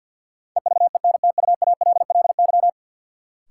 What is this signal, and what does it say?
Morse code, heard as a keyed tone, about 53 wpm, 700 Hz. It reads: EVENTUALLY
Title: Sentences from Top 1000 Words - Review Part 6 - New Word Drill - 50wpm